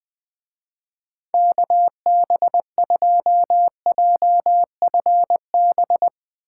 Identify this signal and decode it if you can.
Morse code, heard as a keyed tone, about 20 wpm, 700 Hz. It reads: KB2JFB
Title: Speed Racing - USA 2x3 Call Signs 17wpm